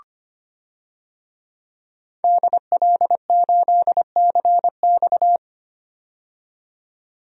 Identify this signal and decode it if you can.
Morse code, heard as a keyed tone, about 25 wpm, 700 Hz. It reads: DL8CX
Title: DXCC-100 Call Signs - 5 Characters - 25wpm